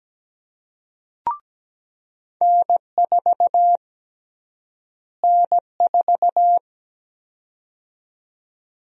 Morse code, heard as a keyed tone, about 17 wpm, 700 Hz. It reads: N4 N4
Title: USA Call Sign Prefixes - Repeated 2x - 17wpm